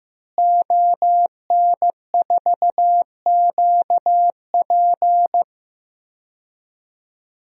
Morse code, heard as a keyed tone, about 15 wpm, 700 Hz. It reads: ON4QP
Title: DXCC-100 Call Signs - 5 Characters - Repeated 2x - 15wpm